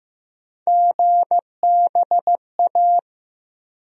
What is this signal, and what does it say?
Morse code, heard as a keyed tone, about 15 wpm, 700 Hz. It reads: GBA
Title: USA Call Sign Postfixes - 3 Letters 15wpm